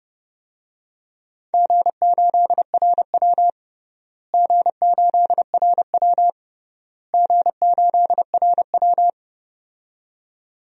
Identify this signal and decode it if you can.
Morse code, heard as a keyed tone, about 30 wpm, 700 Hz. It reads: G8RW G8RW G8RW
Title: DXCC-100 Call Signs - 4 Characters - Repeated 3x - 30wpm